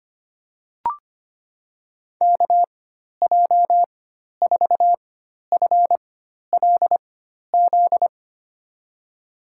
Morse code, heard as a keyed tone, about 25 wpm, spaced as for 12 wpm, 700 Hz. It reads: KJ4FLZ
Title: USA Call Signs - 4x Character Spacing - 25wpm